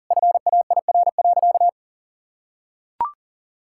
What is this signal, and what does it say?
Morse code, heard as a keyed tone, about 40 wpm, 700 Hz. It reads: FAIR.
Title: Sentences from Top 1600 Words 40wpm